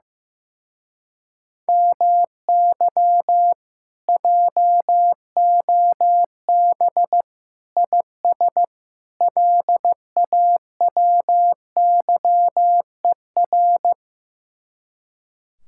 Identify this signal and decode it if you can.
Morse code, heard as a keyed tone, about 15 wpm, 700 Hz. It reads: MY JOB IS LAWYER
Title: Speed-Racing Common QSO Phrases Encore - 15wpm